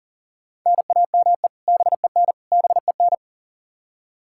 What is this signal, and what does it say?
Morse code, heard as a keyed tone, about 40 wpm, 700 Hz. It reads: NAME BEN BEN
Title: Practice QSO - Exchange 1 - Beginning - 40wpm